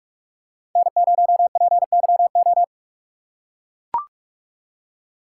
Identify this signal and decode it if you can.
Morse code, heard as a keyed tone, about 41 wpm, 700 Hz. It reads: N0PYO